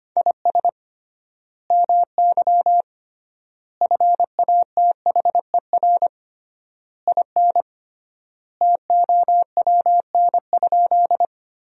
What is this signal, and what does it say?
Morse code, heard as a keyed tone, about 25 wpm, 700 Hz. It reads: IS MY FATHER IN TOWN?
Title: 3x Word-Spacing - Sentences from Top 500 Words - 25wpm